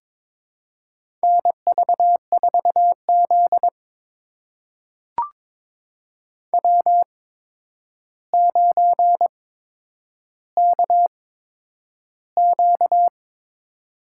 Morse code, heard as a keyed tone, about 22 wpm, spaced as for 9 wpm, 700 Hz. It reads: NV4Z W9KQ
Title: USA Call Signs - 4 Characters - 8x Character Spacing - 22wpm